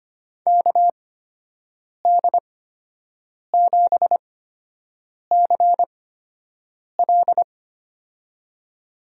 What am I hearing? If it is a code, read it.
Morse code, heard as a keyed tone, about 25 wpm, spaced as for 7 wpm, 700 Hz. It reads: KD7CL
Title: USA Call Signs - 5 Characters - 8x Character Spacing - 25wpm